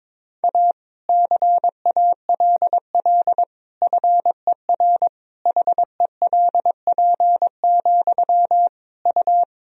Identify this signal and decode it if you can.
Morse code, heard as a keyed tone, about 22 wpm, 700 Hz. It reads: A CALL FER HELP, U
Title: Sentences from Top 300 Words and Abbreviations 22wpm